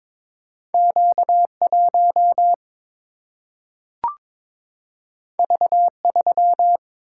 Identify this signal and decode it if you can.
Morse code, heard as a keyed tone, about 22 wpm, 700 Hz. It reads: Q 1 V 3
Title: Single-Letter Single-Number 22wpm